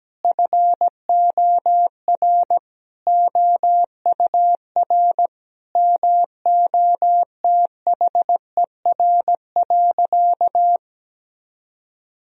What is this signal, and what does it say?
Morse code, heard as a keyed tone, about 17 wpm, 700 Hz. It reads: FOR OUR MOTHER.
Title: Sentences from Top 1000 Words - Review Part 2 - 17wpm